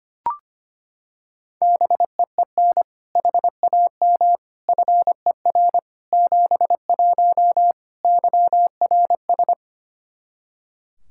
Morse code, heard as a keyed tone, about 25 wpm, 700 Hz. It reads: BEEN HAM FER 71 YRS
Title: Practice QSO - Exchange 3 - More Info - 25wpm